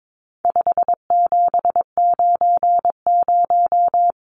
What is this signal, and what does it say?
Morse code, heard as a keyed tone, about 22 wpm, 700 Hz. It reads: 5790
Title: Numbers - 4-Digit 22wpm